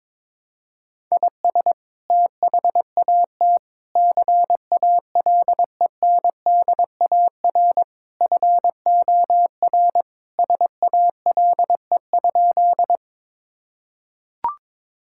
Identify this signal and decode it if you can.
Morse code, heard as a keyed tone, about 22 wpm, 700 Hz. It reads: IS THAT CALENDAR FOR SALE?
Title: Sentences from Top 1600 Words 22wpm